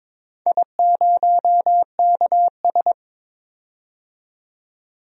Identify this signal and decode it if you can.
Morse code, heard as a keyed tone, about 22 wpm, 700 Hz. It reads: I0KS
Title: DXCC-100 Call Signs - 4 Characters - 22wpm